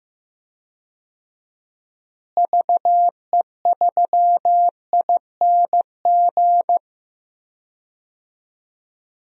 Morse code, heard as a keyed tone, about 15 wpm, 700 Hz. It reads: VE3ING
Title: DXCC-100 Call Signs - 6 Characters - Repeated 3x - 15wpm